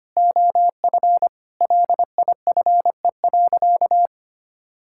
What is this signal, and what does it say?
Morse code, heard as a keyed tone, about 25 wpm, 700 Hz. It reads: OF LIFE.
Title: Sentences from Top 1000 Words - Review Part 6 - 25wpm